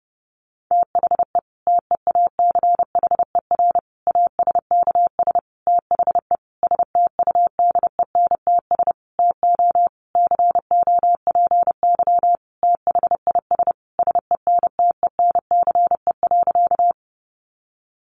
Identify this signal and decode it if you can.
Morse code, heard as a keyed tone, about 30 wpm, 700 Hz. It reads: THE TEACHER ASKS THE STUDENTS TO COPY THIS SENTENCE.
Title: Additional Easy Phrases 30wpm